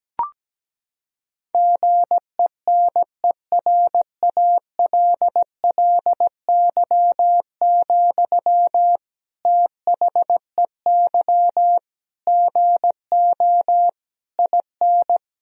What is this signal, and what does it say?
Morse code, heard as a keyed tone, about 17 wpm, 700 Hz. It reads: GENERALLY, THEY GO IN
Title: Sentences from Top 1000 Words - Review Part 4 - 17wpm